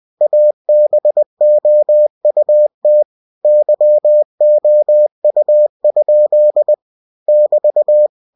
Morse code, heard as a keyed tone, about 20 wpm, 600 Hz. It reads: ABOUT YOU? <BT>
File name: cw-books / winnie-the-pooh / winnie-20wpm-page-0004